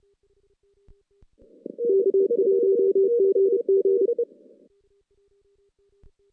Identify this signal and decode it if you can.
Morse code, heard as a keyed tone, about 30 wpm, 390 Hz. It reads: K1GG
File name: 2016 JAN CW NAQP / 2016 Jan naqp k5ot - 0-6